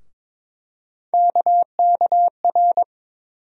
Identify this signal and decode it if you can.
Morse code, heard as a keyed tone, about 22 wpm, 700 Hz. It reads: KKR
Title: USA Call Sign Postfixes - 3 Letters 22wpm